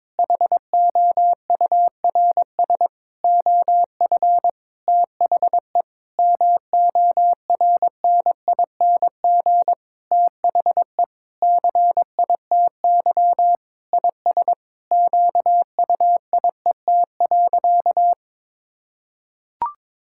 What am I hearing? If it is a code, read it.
Morse code, heard as a keyed tone, about 21 wpm, 700 Hz. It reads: HOURS OF THE MORNING THE CITY IS QUIET.
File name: Sentences with 7-Word Phrases - 22wpm